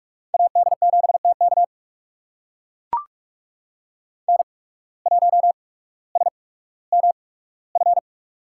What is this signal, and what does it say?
Morse code, heard as a keyed tone, about 44 wpm, spaced as for 19 wpm, 690 Hz. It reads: AD7TK N1SMF
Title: USA Call Signs - 5 Characters - 8x Character Spacing - 45wpm